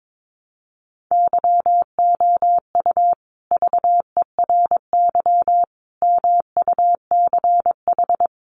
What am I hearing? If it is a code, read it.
Morse code, heard as a keyed tone, about 22 wpm, 700 Hz. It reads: YOU VERY MUCH
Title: Sets of 5 Words Encore 22wpm